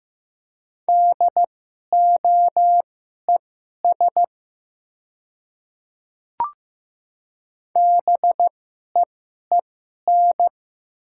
Morse code, heard as a keyed tone, about 15 wpm, spaced as for 11 wpm, 700 Hz. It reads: DOES BEEN
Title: 2x Character Spacing - Top 300 Words - 15wpm